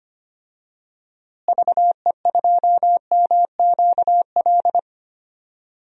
Morse code, heard as a keyed tone, about 25 wpm, 700 Hz. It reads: VE2MQL
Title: DXCC-100 Call Signs - 6 Characters - Repeated 3x - 25wpm